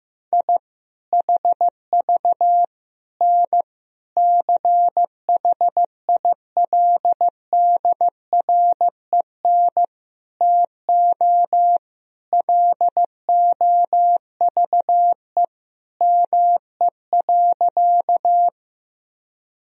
Morse code, heard as a keyed tone, about 15 wpm, 700 Hz. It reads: I HV N CHILDREN TO LOVE ME.
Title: Sentences from Top 400 Words and Abbreviations 15wpm